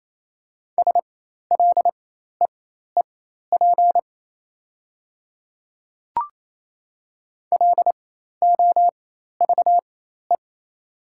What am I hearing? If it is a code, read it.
Morse code, heard as a keyed tone, about 28 wpm, spaced as for 13 wpm, 700 Hz. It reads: SLEEP LOVE
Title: 4x Character Spacing - Top 500 Words - 28wpm